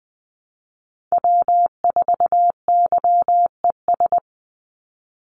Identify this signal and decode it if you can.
Morse code, heard as a keyed tone, about 20 wpm, 700 Hz. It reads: W4YES
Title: Call Signs - 20wpm